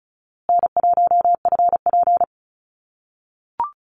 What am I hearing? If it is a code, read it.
Morse code, heard as a keyed tone, about 34 wpm, 700 Hz. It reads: N1FP
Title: USA Call Signs Repeated 3x 35wpm